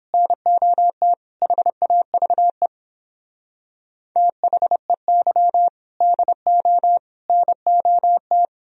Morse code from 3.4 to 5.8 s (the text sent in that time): THEY